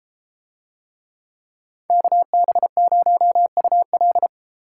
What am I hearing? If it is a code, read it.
Morse code, heard as a keyed tone, about 33 wpm, 700 Hz. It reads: KB0UL